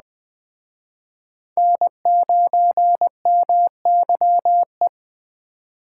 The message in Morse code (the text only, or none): N9MYE